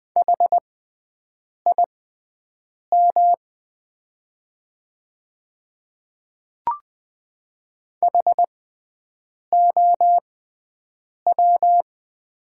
Morse code, heard as a keyed tone, about 20 wpm, spaced as for 7 wpm, 700 Hz. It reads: HIM HOW
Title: 6x Character Spacing - Top 200 Words - 20wpm